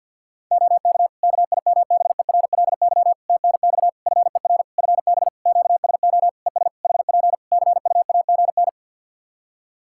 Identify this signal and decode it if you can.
Morse code, heard as a keyed tone, about 52 wpm, 700 Hz. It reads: OK KIMBERLY TNX FER FB QSO ES HP CUAGN
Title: Practice QSO - Exchange 4 - Ending - 50wpm